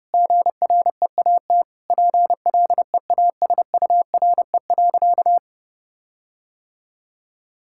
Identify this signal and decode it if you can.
Morse code, heard as a keyed tone, about 30 wpm, 700 Hz. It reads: GREAT PLEASURE.
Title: Sentences from Top 1600 Words 30wpm